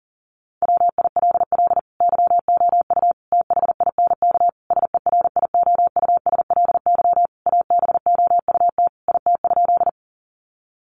Morse code, heard as a keyed tone, about 40 wpm, 700 Hz. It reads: WILL YOU THINK SERIOUSLY ABOUT IT?
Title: Sentences from Top 1200 Words 40wpm